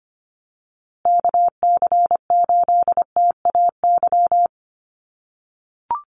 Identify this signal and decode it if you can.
Morse code, heard as a keyed tone, about 25 wpm, 700 Hz. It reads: KC8TAY
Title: USA Call Signs Repeated 2x 25wpm